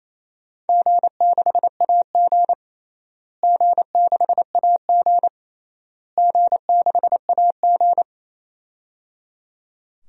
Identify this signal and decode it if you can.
Morse code, heard as a keyed tone, about 28 wpm, 700 Hz. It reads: G6AG G6AG G6AG
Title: DXCC-100 Call Signs - 4 Characters - Repeated 3x - 28wpm